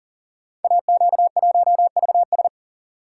Morse code, heard as a keyed tone, about 39 wpm, 690 Hz. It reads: AQ1VS
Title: Code Groups - 5 Letter:Numbers - 40wpm